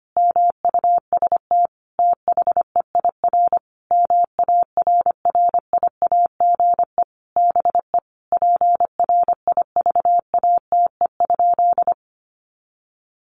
Morse code, heard as a keyed tone, about 25 wpm, 700 Hz. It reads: MUST THEIR MARRIAGE BE PRIVATE?
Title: Sentences from Top 1000 Words 25wpm